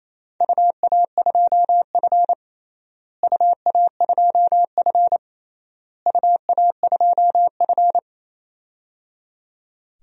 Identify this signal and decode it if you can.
Morse code, heard as a keyed tone, about 28 wpm, 700 Hz. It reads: UA2F UA2F UA2F
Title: DXCC-100 Call Signs - 4 Characters - Repeated 3x - 28wpm